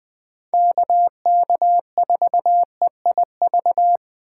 Morse code, heard as a keyed tone, about 24 wpm, 700 Hz. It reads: KK4EIV
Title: Speed Racing - USA 2x3 Call Signs 20wpm